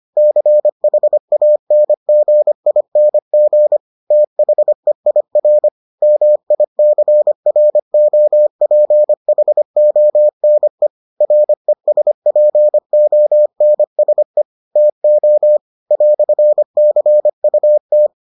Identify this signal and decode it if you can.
Morse code, heard as a keyed tone, about 25 wpm, 600 Hz. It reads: CHANGING THEIR MICROPHONE RESPONSE TO "CUT